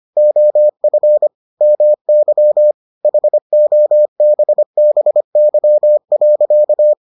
Morse code, heard as a keyed tone, about 25 wpm, 600 Hz. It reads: OF MY HOBBY.